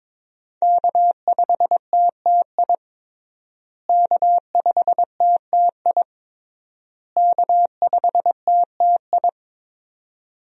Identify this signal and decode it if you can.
Morse code, heard as a keyed tone, about 22 wpm, 700 Hz. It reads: K5TTI K5TTI K5TTI
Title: USA Call Signs - 5 Characters - Repeated 3x - 22wpm